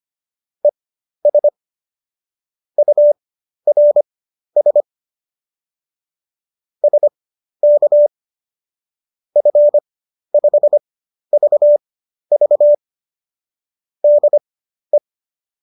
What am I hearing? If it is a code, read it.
Morse code, heard as a keyed tone, about 25 wpm, spaced as for 12 wpm, 590 Hz. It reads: ES URS SK F5VV DE
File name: qso009 12